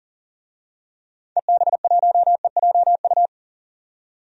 Morse code, heard as a keyed tone, about 40 wpm, 700 Hz. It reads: EB1EJU
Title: DXCC-100 Call Signs - 6 Characters - 40wpm